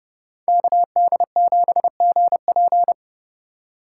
Morse code, heard as a keyed tone, about 31 wpm, 700 Hz. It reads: KD7GP